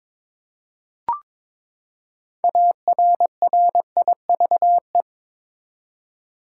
Morse code, heard as a keyed tone, about 22 wpm, 700 Hz. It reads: ARRIVE
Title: Six Letter Words - 22wpm